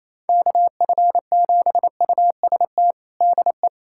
Morse code, heard as a keyed tone, about 28 wpm, 700 Hz. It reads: KF7UST DE
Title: USA Callsign DE Callsign 28wpm